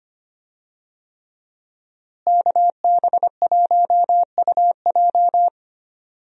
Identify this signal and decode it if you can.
Morse code, heard as a keyed tone, about 25 wpm, 700 Hz. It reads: KB1UJ
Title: USA Call Signs - 5 Characters - Repeated 3x - 25wpm